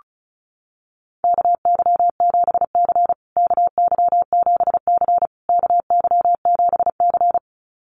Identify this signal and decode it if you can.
Morse code, heard as a keyed tone, about 34 wpm, 700 Hz. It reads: KY7C KY7C KY7C
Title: USA Call Signs Repeated 3x 35wpm